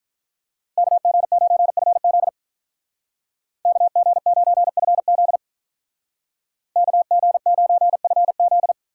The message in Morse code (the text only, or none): KG9FZ KG9FZ KG9FZ